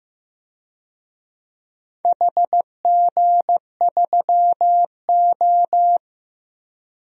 Morse code, heard as a keyed tone, about 15 wpm, 700 Hz. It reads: HG3O
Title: DXCC-100 Call Signs - 4 Characters - Repeated 3x - 15wpm